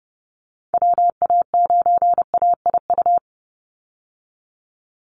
Morse code, heard as a keyed tone, about 30 wpm, 700 Hz. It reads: WA9AIU
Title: USA Call Signs 30wpm